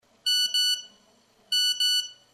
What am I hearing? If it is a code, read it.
There is no code.